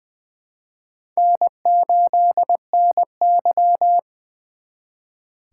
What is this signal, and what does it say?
Morse code, heard as a keyed tone, about 24 wpm, 700 Hz. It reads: N8NY
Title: Speed Racing - USA 1x2 Call Signs 20wpm